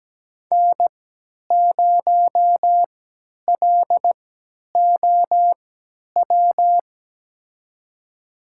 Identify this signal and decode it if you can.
Morse code, heard as a keyed tone, about 17 wpm, spaced as for 10 wpm, 700 Hz. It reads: N0LOW
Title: USA Call Signs - 5 Characters - 3x Character Spacing - 17wpm